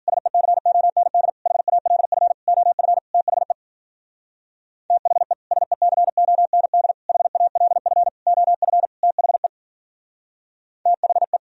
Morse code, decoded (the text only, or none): SECOND HALF OF THE THE SECOND HALF OF THE THE